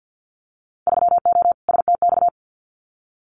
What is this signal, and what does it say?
Morse code, heard as a keyed tone, about 50 wpm, 700 Hz. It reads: 3O STX
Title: Winter Field Day Exchanges 50wpm